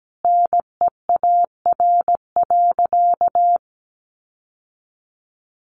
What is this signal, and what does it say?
Morse code, heard as a keyed tone, about 17 wpm, 700 Hz. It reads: NEAR.
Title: Sentences from Top 700 Words 17wpm